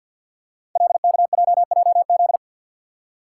Morse code, heard as a keyed tone, about 53 wpm, 700 Hz. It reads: LKJJZ
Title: Code Groups - 5 Letters - 50wpm